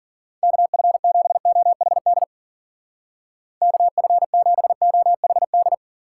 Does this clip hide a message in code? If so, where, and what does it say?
Yes: Morse code from 0.4 to 5.7 s, heard as a keyed tone, about 52 wpm, 700 Hz. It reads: KF7OHD KF7OHD